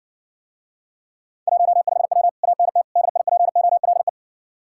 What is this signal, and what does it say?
Morse code, heard as a keyed tone, about 61 wpm, 700 Hz. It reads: 15W ANT DIPOLE